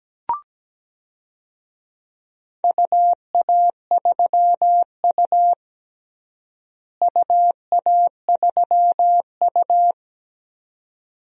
Morse code, heard as a keyed tone, about 17 wpm, 700 Hz. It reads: UA3U UA3U
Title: DXCC-100 Call Signs - 4 Characters - Repeated 2x - 17wpm